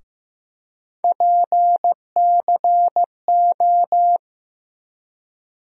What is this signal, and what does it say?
Morse code, heard as a keyed tone, about 15 wpm, 700 Hz. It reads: PCO
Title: USA Call Sign Postfixes - 3 Letters 15wpm